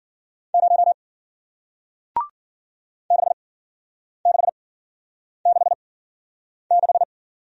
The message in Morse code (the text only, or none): 0 6666